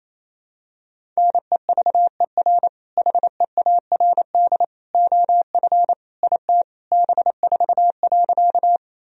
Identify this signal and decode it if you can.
Morse code, heard as a keyed tone, about 28 wpm, 700 Hz. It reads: NEVER HEARD OF IT B4.
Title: Sentences from Top 400 Words and Abbreviations 28wpm